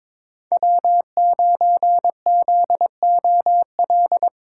Morse code, heard as a keyed tone, about 21 wpm, 700 Hz. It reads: W9ZOL
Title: USA Call Signs - 5 Characters - 3x Character Spacing - 22wpm